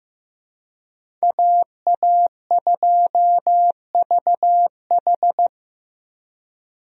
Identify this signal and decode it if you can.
Morse code, heard as a keyed tone, about 15 wpm, 700 Hz. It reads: AA2VH
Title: USA Call Signs - 5 Characters - Repeated 3x - 15wpm